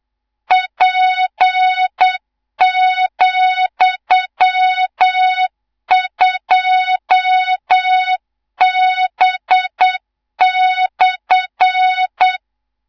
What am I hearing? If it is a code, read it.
Morse code, heard as a keyed tone, about 8 wpm, 3700 Hz, with noise. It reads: P,2B/
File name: Morze-HA2MN / hang / 18 IRASJELEK2